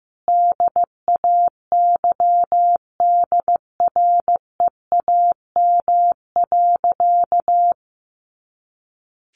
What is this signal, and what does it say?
Morse code, heard as a keyed tone, about 15 wpm, 700 Hz. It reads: DAYDREAM.
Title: Easy Phrases 15wpm